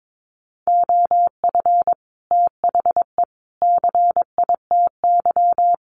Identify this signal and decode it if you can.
Morse code, heard as a keyed tone, about 22 wpm, 700 Hz. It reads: OF THE CITY
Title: Sets of 3 Words 22wpm